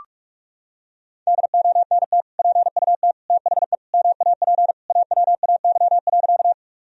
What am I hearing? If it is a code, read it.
Morse code, heard as a keyed tone, about 44 wpm, 700 Hz. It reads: DONT PUT THE MAP AWAY.
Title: Sentences from Top 1000 Words - Review Part 2 - 45wpm